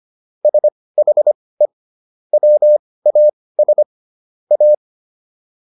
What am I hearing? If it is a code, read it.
Morse code, heard as a keyed tone, about 25 wpm, spaced as for 18 wpm, 600 Hz. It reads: SHE WAS A